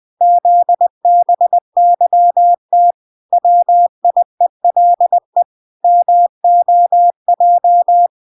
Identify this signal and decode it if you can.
Morse code, heard as a keyed tone, about 20 wpm, 700 Hz. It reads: ZBYT WIELE MOJ